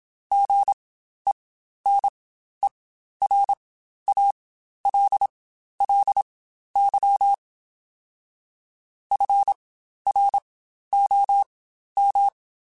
Morse code, heard as a keyed tone, about 27 wpm, spaced as for 12 wpm, 800 Hz. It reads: GENERALLY FROM
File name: Spaced Characters / Spaced Characters-mp3s / Weather 5 22 sp @25